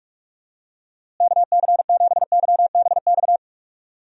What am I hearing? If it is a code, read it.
Morse code, heard as a keyed tone, about 47 wpm, 700 Hz. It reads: KC7YBX
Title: USA Call Signs - 6 Characters - Repeated 3x - 45wpm